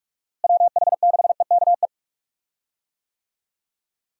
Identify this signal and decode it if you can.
Morse code, heard as a keyed tone, about 46 wpm, 700 Hz. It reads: WH6EKE
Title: USA Call Signs - 6 Characters - Repeated 2x - 45wpm